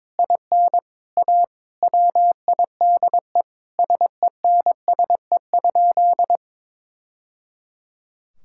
Morse code, heard as a keyed tone, about 22 wpm, 700 Hz. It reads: IN A WIDE SENSE?
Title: Sentences from Top 1000 Words - Review Part 6 - 22wpm